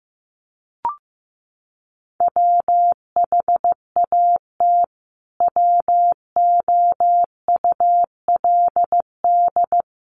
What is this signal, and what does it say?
Morse code, heard as a keyed tone, about 15 wpm, 700 Hz. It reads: WHAT WOULD
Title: Sentences from Top 300 Words 15wpm